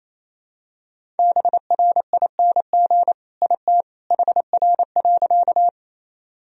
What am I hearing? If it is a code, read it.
Morse code, heard as a keyed tone, about 28 wpm, 700 Hz. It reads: BRING IT HR.
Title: Sentences from Top 500 Words and Abbreviations 28wpm